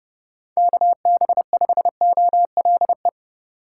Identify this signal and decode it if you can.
Morse code, heard as a keyed tone, about 32 wpm, 700 Hz. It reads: KB5OLE